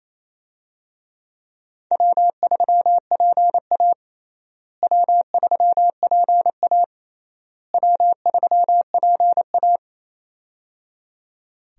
Morse code, heard as a keyed tone, about 27 wpm, 700 Hz. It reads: W3PA W3PA W3PA
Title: USA Call Signs - 4 Characters - Repeated 3x - 28wpm